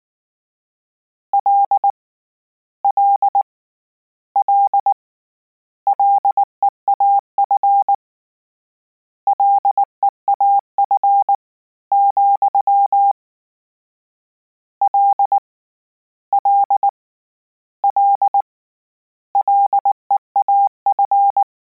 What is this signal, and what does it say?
Morse code, heard as a keyed tone, about 19 wpm, 800 Hz. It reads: L L L LEAF LEAF, L L L LEAF